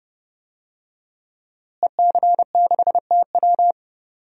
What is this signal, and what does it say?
Morse code, heard as a keyed tone, about 30 wpm, 700 Hz. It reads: EC6TW
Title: DXCC-100 Call Signs - 5 Characters - Repeated 3x - 30wpm